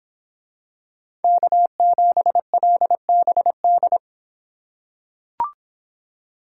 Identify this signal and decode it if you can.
Morse code, heard as a keyed tone, about 25 wpm, 700 Hz. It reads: K7LBD